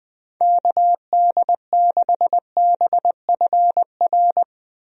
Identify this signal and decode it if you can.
Morse code, heard as a keyed tone, about 21 wpm, 700 Hz. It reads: KD6BFR
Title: USA Call Signs - 6 Characters - Repeated 2x - 20wpm